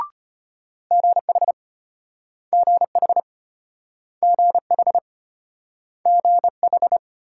Morse code, heard as a keyed tone, about 34 wpm, 700 Hz. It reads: GH GH GH GH